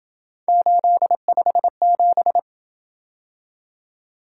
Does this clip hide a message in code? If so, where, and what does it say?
Yes: Morse code from 0.5 to 2.4 s, heard as a keyed tone, about 26 wpm, 700 Hz. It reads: 857